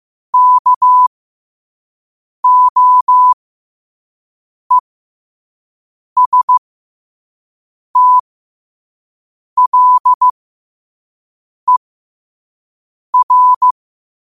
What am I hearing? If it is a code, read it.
Morse code, heard as a keyed tone, about 15 wpm, spaced as for 5 wpm, 1000 Hz. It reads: KOESTLER
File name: qotdfiles / 5wpm